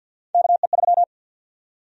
Morse code, heard as a keyed tone, about 51 wpm, 700 Hz. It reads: KE3
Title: USA Call Sign Prefixes - 50wpm